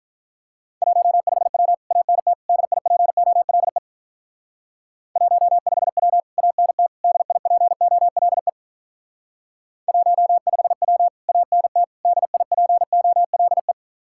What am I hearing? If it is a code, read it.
Morse code, heard as a keyed tone, about 44 wpm, 700 Hz. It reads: 15W ANT DIPOLE 15W ANT DIPOLE 15W ANT DIPOLE